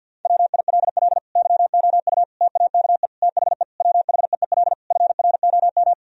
Morse code, heard as a keyed tone, about 50 wpm, 700 Hz. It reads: WILL YOU TAKE THE WHEEL FROM
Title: Sentences from Top 1000 Words - Review Part 3 - 50wpm